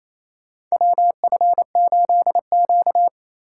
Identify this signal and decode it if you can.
Morse code, heard as a keyed tone, about 28 wpm, 700 Hz. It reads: WF8Q
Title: USA Call Signs - 4 Characters - 4x Character Spacing - 28wpm